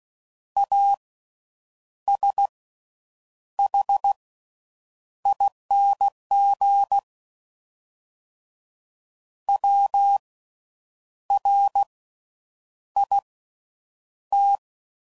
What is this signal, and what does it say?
Morse code, heard as a keyed tone, about 16 wpm, spaced as for 7 wpm, 800 Hz. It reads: ASHING WRIT
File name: Slow With Faster Ings